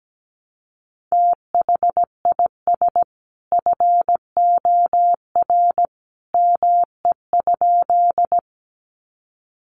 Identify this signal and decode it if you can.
Morse code, heard as a keyed tone, about 17 wpm, 700 Hz. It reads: THIS FOR ME?
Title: Sentences from Top 200 Words 17wpm